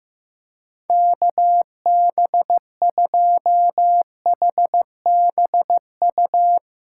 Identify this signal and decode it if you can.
Morse code, heard as a keyed tone, about 15 wpm, 700 Hz. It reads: KB2HBU
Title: USA Call Signs - 6 Characters - 6x Character Spacing - 15wpm